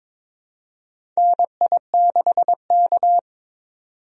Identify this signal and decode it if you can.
Morse code, heard as a keyed tone, about 22 wpm, 700 Hz. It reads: NI6K
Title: USA Call Signs - 4 Characters - 22wpm